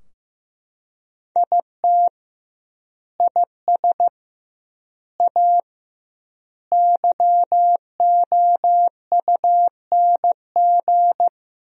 Morse code, heard as a keyed tone, about 15 wpm, 700 Hz. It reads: IT IS A YOUNG
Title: 2x Word-Spacing - Sentences from Top 400 Words - 15wpm